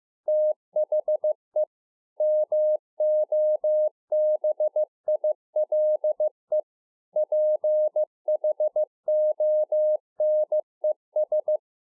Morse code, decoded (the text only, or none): THE MOBILE PHONES